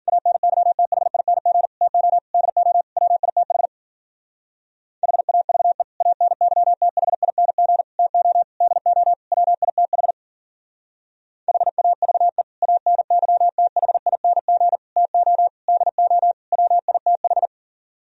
Morse code, heard as a keyed tone, about 52 wpm, 700 Hz. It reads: ANYTHING TO DO WITH HAVE ANYTHING TO DO WITH HAVE ANYTHING TO DO WITH